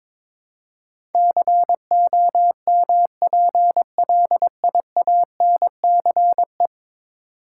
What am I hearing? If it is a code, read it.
Morse code, heard as a keyed tone, about 22 wpm, 700 Hz. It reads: COMPLIANCE
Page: morse-code-ninja-practice-22wpm